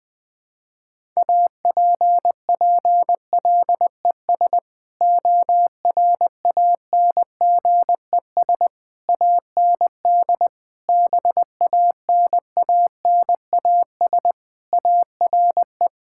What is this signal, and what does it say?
Morse code, heard as a keyed tone, about 20 wpm, 700 Hz. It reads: APPLES ORANGES AND BANANAS ARE
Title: Sentences with 6-Word Phrases - Spoken Beginning - 20wpm